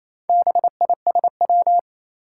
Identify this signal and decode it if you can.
Morse code, heard as a keyed tone, about 27 wpm, 700 Hz. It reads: BISW
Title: Four Challenging Letter-Number Combinations 28wpm